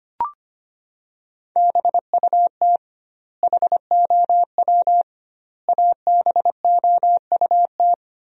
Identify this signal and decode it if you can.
Morse code, heard as a keyed tone, about 25 wpm, 700 Hz. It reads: BUT HOW ABOUT
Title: 2x Word-Spacing - Sentences from Top 100 Words - 25wpm